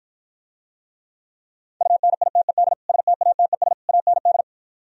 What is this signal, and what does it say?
Morse code, heard as a keyed tone, about 53 wpm, 700 Hz. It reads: UNITED STATES AND